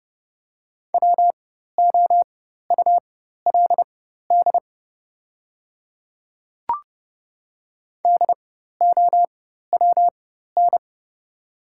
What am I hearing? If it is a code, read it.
Morse code, heard as a keyed tone, about 30 wpm, spaced as for 14 wpm, 700 Hz. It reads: WOULD DOWN
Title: 4x Character Spacing - Top 300 Words - 30wpm